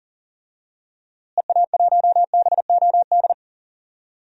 Morse code, heard as a keyed tone, about 40 wpm, 700 Hz. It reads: EA1BOD
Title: DXCC-100 Call Signs - 6 Characters - Repeated 2x - 40wpm